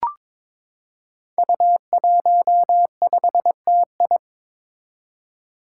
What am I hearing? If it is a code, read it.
Morse code, heard as a keyed tone, about 22 wpm, 700 Hz. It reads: U15TI
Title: Five Challenging Letter-Number Combinations 22wpm